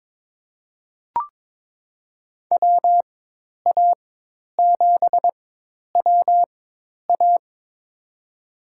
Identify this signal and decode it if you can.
Morse code, heard as a keyed tone, about 22 wpm, spaced as for 10 wpm, 700 Hz. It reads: WA7WA
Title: USA Call Signs - 5 Characters - 4x Character Spacing - 22wpm